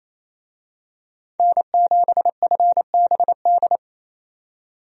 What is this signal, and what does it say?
Morse code, heard as a keyed tone, about 28 wpm, 700 Hz. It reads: N7FBD
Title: USA Call Signs - 5 Characters - 28wpm